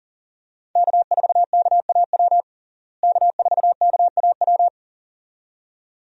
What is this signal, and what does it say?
Morse code, heard as a keyed tone, about 40 wpm, 700 Hz. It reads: K4KAW K4KAW
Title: USA Call Signs - 5 Characters - Repeated 2x - 40wpm